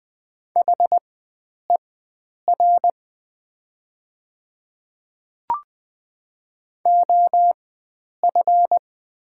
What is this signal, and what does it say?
Morse code, heard as a keyed tone, about 20 wpm, spaced as for 9 wpm, 700 Hz. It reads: HER OF
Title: 4x Character Spacing - Top 300 Words - 20wpm